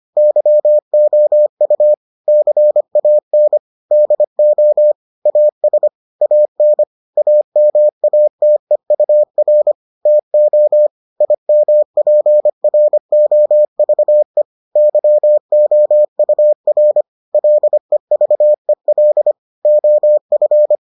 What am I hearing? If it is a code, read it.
Morse code, heard as a keyed tone, about 25 wpm, 600 Hz. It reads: YOU CAN DO AS AN AMATEUR TO IMPROVE YOUR LEVEL OF